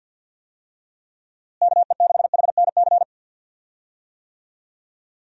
Morse code, heard as a keyed tone, about 51 wpm, 700 Hz. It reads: KE6HNC
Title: USA Call Signs - 6 Characters - Repeated 2x - 50wpm